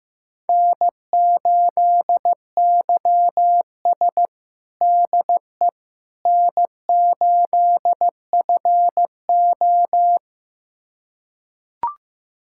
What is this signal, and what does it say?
Morse code, heard as a keyed tone, about 15 wpm, 700 Hz. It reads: N8YS DE N8FO
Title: USA Callsign DE Callsign - 4 Characters - 15wpm